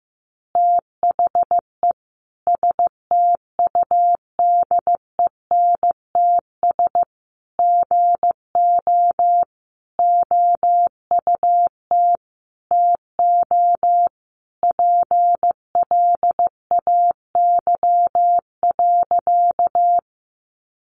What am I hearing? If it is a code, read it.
Morse code, heard as a keyed tone, about 14 wpm, 700 Hz. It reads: THE STUDENTS GO OUT TO PLAY.
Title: Easy Phrases 15wpm